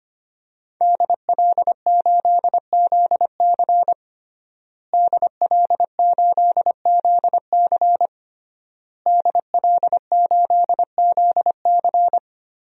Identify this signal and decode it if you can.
Morse code, heard as a keyed tone, about 25 wpm, 700 Hz. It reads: DL8ZC DL8ZC DL8ZC
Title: DXCC-100 Call Signs - 5 Characters - Repeated 3x - 25wpm